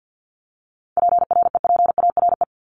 Morse code, heard as a keyed tone, about 50 wpm, 700 Hz. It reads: PREPARE
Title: ED Suffix - Top 100 Words with Base Words 50wpm